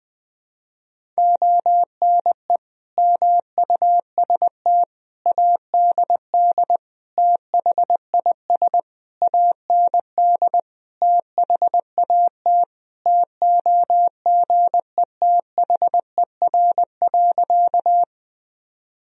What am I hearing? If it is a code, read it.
Morse code, heard as a keyed tone, about 20 wpm, 700 Hz. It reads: ONE MUST ADD THIS AND THAT TOGETHER.
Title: Sentences from Top 1000 Words - Review Part 1 - 20wpm